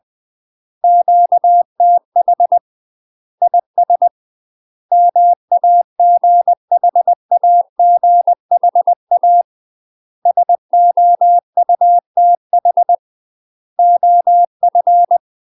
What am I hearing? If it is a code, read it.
Morse code, heard as a keyed tone, about 20 wpm, 700 Hz. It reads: QTH IS MAGHAGHA SOUTH OF
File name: UK MASTER QSOs TRAINING / 20WPM / UK MASTER QSOs 0058